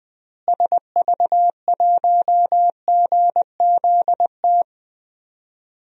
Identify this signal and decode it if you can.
Morse code, heard as a keyed tone, about 20 wpm, 700 Hz. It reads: SV1GZT
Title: DXCC-100 Call Signs - 6 Characters - Repeated 3x - 20wpm